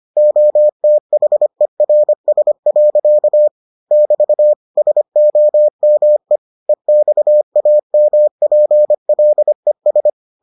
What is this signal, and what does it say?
Morse code, heard as a keyed tone, about 25 wpm, 600 Hz. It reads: OTHERS. <BT> SOME EXAMPLES